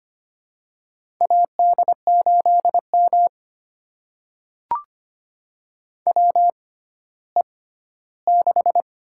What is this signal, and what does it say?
Morse code, heard as a keyed tone, about 25 wpm, spaced as for 14 wpm, 700 Hz. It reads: AD8M WE6
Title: USA Call Signs - 4 Characters - 6x Character Spacing - 25wpm